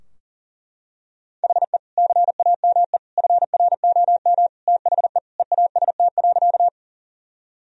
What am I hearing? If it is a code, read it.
Morse code, heard as a keyed tone, about 40 wpm, 700 Hz. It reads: HE CAME FROM THE EAST.
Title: Sentences from Top 1000 Words - Review Part 3 - 40wpm